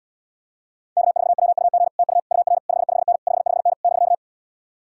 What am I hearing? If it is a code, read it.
Morse code, heard as a keyed tone, about 77 wpm, 700 Hz. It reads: K4WUG TU UR 55N 55N <BK>